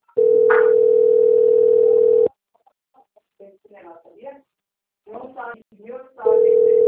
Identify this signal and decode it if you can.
Ringback tone (North America), 440 + 480 Hz, with speech.
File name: out- - - - - .53230